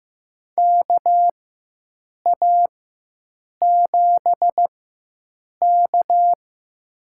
Morse code, heard as a keyed tone, about 15 wpm, spaced as for 7 wpm, 700 Hz. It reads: KA7K
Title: USA Call Signs - 4 Characters - 4x Character Spacing - 15wpm